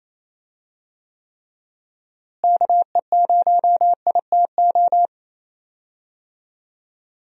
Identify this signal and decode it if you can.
Morse code, heard as a keyed tone, about 28 wpm, 700 Hz. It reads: KE0ITO
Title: USA Call Signs - 6 Characters - Repeated 2x - 28wpm